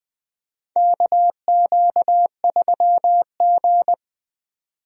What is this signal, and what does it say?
Morse code, heard as a keyed tone, about 20 wpm, 700 Hz. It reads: KQ3G
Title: USA Call Signs - 4 Characters - 20wpm